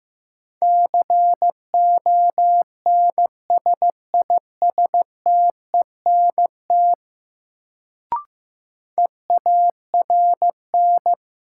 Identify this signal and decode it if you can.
Morse code, heard as a keyed tone, about 15 wpm, 700 Hz. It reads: CONSISTENT EARN
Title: Sentences from Top 1500 Words - New Word Drill Alt - 15wpm